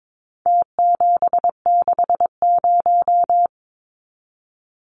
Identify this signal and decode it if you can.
Morse code, heard as a keyed tone, about 22 wpm, 700 Hz. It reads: T760
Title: Numbers - Contest Style - 4-Digit 22wpm